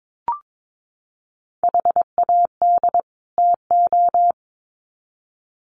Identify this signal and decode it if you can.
Morse code, heard as a keyed tone, about 22 wpm, 700 Hz. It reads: HAD TO
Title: Sets of 2 Words 22wpm